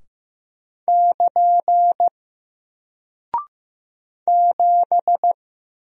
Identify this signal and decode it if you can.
Morse code, heard as a keyed tone, about 15 wpm, 700 Hz. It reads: <KN> 7
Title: Single Character (Letter:Number:Punctuation:Prosign) Alt 15wpm